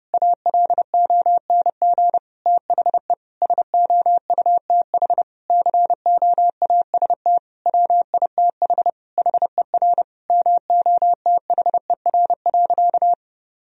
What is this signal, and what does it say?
Morse code, heard as a keyed tone, about 30 wpm, 700 Hz. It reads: ALONG THE SOUTH COAST WITH HER MOTHER.
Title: Sentences from Top 1000 Words - Review Part 4 - 30wpm